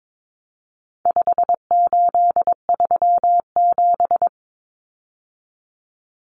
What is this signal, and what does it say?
Morse code, heard as a keyed tone, about 22 wpm, 700 Hz. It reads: 5837
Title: Numbers - 4-Digit 22wpm